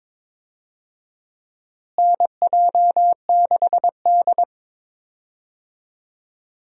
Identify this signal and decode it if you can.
Morse code, heard as a keyed tone, about 22 wpm, 700 Hz. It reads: NJ6D
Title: USA Call Signs - 4 Characters - Repeated 2x - 22wpm